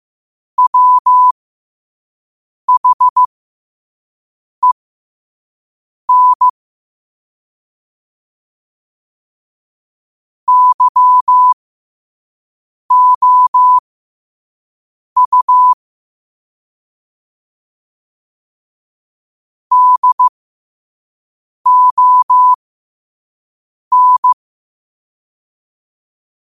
Morse code, heard as a keyed tone, about 15 wpm, spaced as for 5 wpm, 1000 Hz. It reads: WHEN YOU DON
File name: qotdfiles / 5wpm